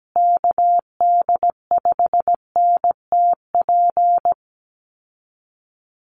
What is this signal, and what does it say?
Morse code, heard as a keyed tone, about 17 wpm, 700 Hz. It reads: KD5NTP
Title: USA Call Signs Repeated 2x 17wpm